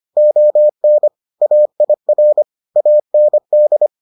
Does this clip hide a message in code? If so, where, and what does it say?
Yes: Morse code from 0.2 to 3.9 s, heard as a keyed tone, about 25 wpm, 600 Hz. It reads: ON AIR AND